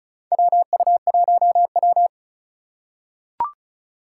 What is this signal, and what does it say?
Morse code, heard as a keyed tone, about 36 wpm, 700 Hz. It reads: WU1W
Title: USA Call Signs - 4 Characters - 3x Character Spacing - 35wpm